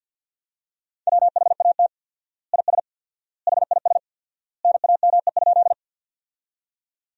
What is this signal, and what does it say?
Morse code, heard as a keyed tone, about 53 wpm, 700 Hz. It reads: WHAT IS HIS NAME?
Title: 4x Word-Spacing - Sentences from Top 200 Words - 50wpm